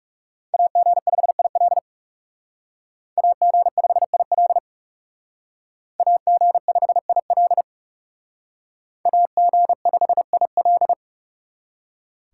Morse code, heard as a keyed tone, about 39 wpm, 700 Hz. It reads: AG5IL AG5IL AG5IL AG5IL